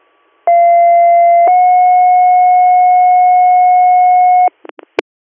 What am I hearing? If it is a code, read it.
Two-tone page, heard as a one-second tone, then a three-second tone, about 690 then 730 Hz. It is heard from 0.5 to 4.5 s.